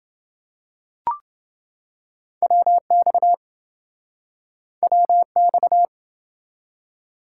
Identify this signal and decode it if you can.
Morse code, heard as a keyed tone, about 28 wpm, 700 Hz. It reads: WX WX